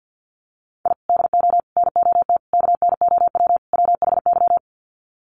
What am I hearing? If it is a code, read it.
Morse code, heard as a keyed tone, about 50 wpm, 700 Hz. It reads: I DO NOT KNOW WHY
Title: Sets of 5 Words 50wpm